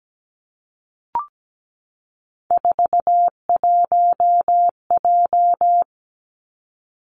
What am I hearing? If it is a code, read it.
Morse code, heard as a keyed tone, about 17 wpm, 700 Hz. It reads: 41J
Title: Three Challenging Letter-Number Combinations 17wpm